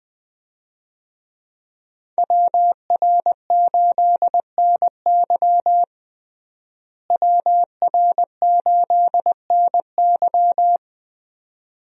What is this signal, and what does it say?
Morse code, heard as a keyed tone, about 20 wpm, 700 Hz. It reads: WR8NY WR8NY
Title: USA Call Signs - 5 Characters - Repeated 2x - 20wpm